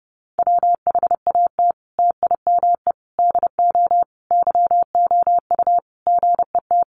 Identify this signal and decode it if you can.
Morse code, heard as a keyed tone, about 30 wpm, 700 Hz. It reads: WHAT TIME DO YOU GET
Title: Sentences from Top 1500 Words 30wpm